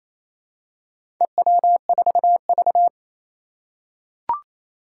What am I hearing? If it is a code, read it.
Morse code, heard as a keyed tone, about 27 wpm, 700 Hz. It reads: EW4V
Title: Four Challenging Letter-Number Combinations 28wpm